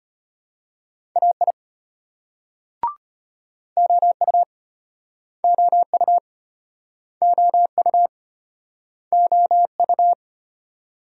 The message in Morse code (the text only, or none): AI OU OU OU OU